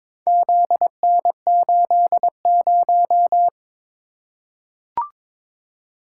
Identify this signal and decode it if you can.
Morse code, heard as a keyed tone, about 22 wpm, 700 Hz. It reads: ZN80
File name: practice / 22wpm / Four Letter-Number Combinations - 22wpm